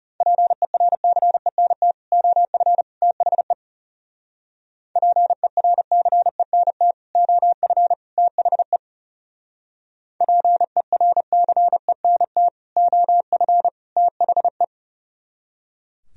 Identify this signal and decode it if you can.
Morse code, heard as a keyed tone, about 38 wpm, 700 Hz. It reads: PERCENT OF THE PERCENT OF THE PERCENT OF THE
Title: Speed-Racing - Sets of 3 Words 30wpm